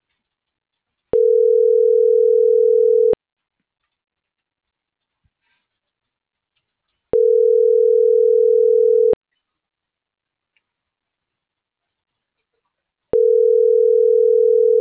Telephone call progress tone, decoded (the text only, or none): ringback tone